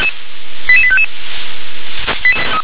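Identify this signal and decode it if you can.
Five-tone selective call (ZVEI), heard as quick runs of five tones, tones of about 70 ms. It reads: 88855
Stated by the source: Five-tone ZVEI selective call received by SDR, code 88152.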